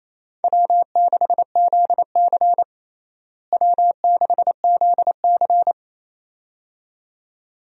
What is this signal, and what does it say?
Morse code, heard as a keyed tone, about 28 wpm, 700 Hz. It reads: W6ZC W6ZC
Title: USA Call Signs - 4 Characters - Repeated 2x - 28wpm